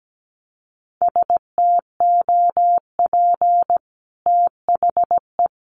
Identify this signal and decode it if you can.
Morse code, heard as a keyed tone, about 17 wpm, 700 Hz. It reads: STOP THE
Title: Sentences from Top 1200 Words 17wpm